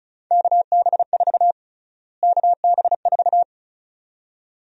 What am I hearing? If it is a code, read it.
Morse code, heard as a keyed tone, about 36 wpm, 700 Hz. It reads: KB4 KB4
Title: USA Call Sign Prefixes - Repeated 2x - 35wpm